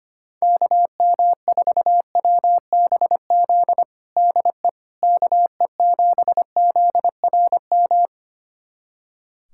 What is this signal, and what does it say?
Morse code, heard as a keyed tone, about 25 wpm, 700 Hz. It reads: KM4WBZ DE KE7ZRM
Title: USA Callsign DE Callsign 25wpm